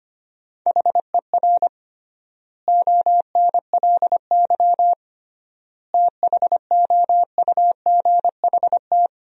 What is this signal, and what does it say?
Morse code, heard as a keyed tone, about 25 wpm, 700 Hz. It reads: HER ONLY THOUGHT
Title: 3x Word-Spacing - Sentences from Top 400 Words - 25wpm